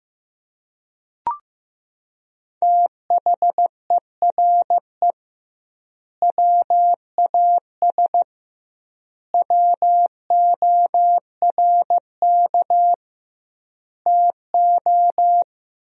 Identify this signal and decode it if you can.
Morse code, heard as a keyed tone, about 16 wpm, 700 Hz. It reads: THERE WAS WORK TO
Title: 2x Word-Spacing - Sentences from Top 200 Words - 15wpm